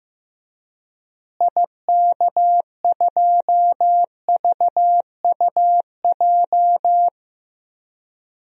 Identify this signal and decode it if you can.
Morse code, heard as a keyed tone, about 15 wpm, 700 Hz. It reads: IK2VUJ
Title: DXCC-100 Call Signs - 6 Characters - Repeated 3x - 15wpm